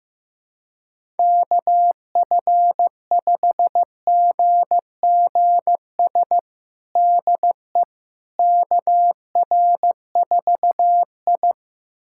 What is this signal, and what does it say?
Morse code, heard as a keyed tone, about 15 wpm, 700 Hz. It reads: KF5GGS DE KR4I
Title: USA Callsign DE Callsign 15wpm